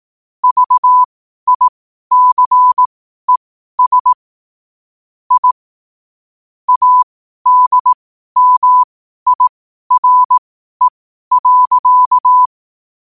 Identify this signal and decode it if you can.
Morse code, heard as a keyed tone, about 18 wpm, spaced as for 12 wpm, 1000 Hz. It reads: VICES I ADMIRE.
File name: qotdfiles / 12wpm